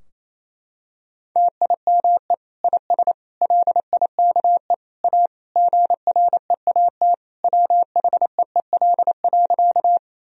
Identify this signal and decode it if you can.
Morse code, heard as a keyed tone, about 28 wpm, 700 Hz. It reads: TIME IS LIKE A GREAT WHEEL.
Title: Sentences from Top 1000 Words - Review Part 3 - 28wpm